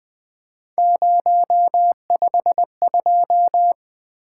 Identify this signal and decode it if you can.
Morse code, heard as a keyed tone, about 20 wpm, 700 Hz. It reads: 052
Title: Three Letter-Number Combinations - 20wpm